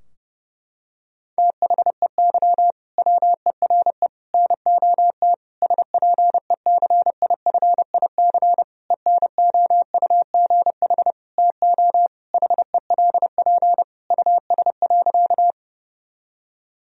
Morse code, heard as a keyed tone, about 32 wpm, 700 Hz. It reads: THEY WERE NOT SPECIFIC ENOUGH TO HELP US.
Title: Sentences from Top 1000 Words - Review Part 3 - 30wpm